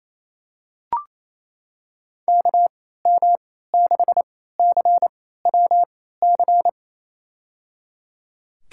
Morse code, heard as a keyed tone, about 28 wpm, spaced as for 16 wpm, 700 Hz. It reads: KM6CWC
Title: USA Call Signs - 6 Characters - 3x Character Spacing - 28wpm